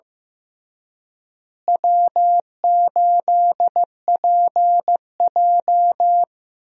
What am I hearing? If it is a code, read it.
Morse code, heard as a keyed tone, about 15 wpm, 700 Hz. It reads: W8PJ
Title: USA Call Signs - 4 Characters - Repeated 3x - 15wpm